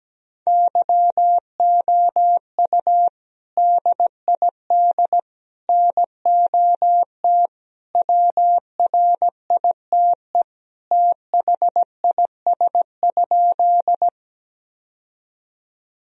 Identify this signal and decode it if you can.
Morse code, heard as a keyed tone, about 17 wpm, 700 Hz. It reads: YOU DID NOT WRITE THIS?
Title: Sentences from Top 1000 Words - Review Part 1 - 17wpm